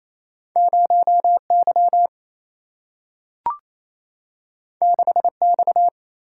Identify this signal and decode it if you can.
Morse code, heard as a keyed tone, about 28 wpm, 700 Hz. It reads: 0 Y 6 X
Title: Single-Letter Single-Number 28wpm